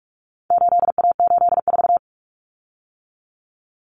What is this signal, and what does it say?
Morse code, heard as a keyed tone, about 44 wpm, 700 Hz. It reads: 8A84
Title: Common Cut Numbers - 4-Digit 45wpm